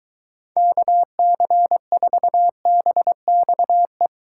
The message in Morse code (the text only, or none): KC4BXE